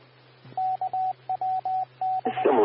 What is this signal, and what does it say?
Morse code, heard as a keyed tone, about 20 wpm, 750 Hz, with radio noise. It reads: KWB
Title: W6NUT 147.450 MHz Repeater